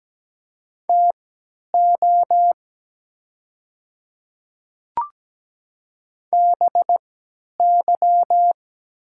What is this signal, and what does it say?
Morse code, heard as a keyed tone, about 17 wpm, spaced as for 10 wpm, 700 Hz. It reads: TO BY
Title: 3x Character Spacing - Top 100 Words - 17wpm